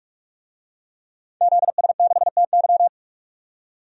Morse code, heard as a keyed tone, about 46 wpm, 700 Hz. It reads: ZS6TY
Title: DXCC-100 Call Signs - 5 Characters - 45wpm